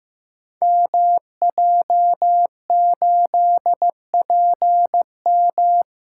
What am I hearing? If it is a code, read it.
Morse code, heard as a keyed tone, about 15 wpm, 700 Hz. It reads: MJ8PM
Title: Code Groups - 5 Letter:Numbers - 15wpm